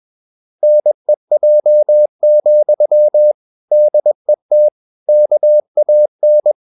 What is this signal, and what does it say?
Morse code, heard as a keyed tone, about 21 wpm, 600 Hz. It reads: NEJ, DET KAN
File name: Ferie QRS 21wpm